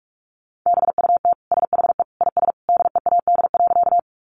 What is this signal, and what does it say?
Morse code, heard as a keyed tone, about 44 wpm, 700 Hz. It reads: BUT SHE IS DEAD.
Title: Sentences from Top 1500 Words 45wpm